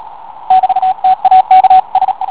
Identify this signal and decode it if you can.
Morse code, heard as a keyed tone, about 33 wpm, 750 Hz, with radio noise. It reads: <BT> TAKS